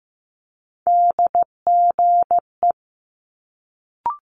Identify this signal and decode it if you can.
Morse code, heard as a keyed tone, about 15 wpm, 700 Hz. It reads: DGE
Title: Multi-letter Phonemes Quick Fire 15wpm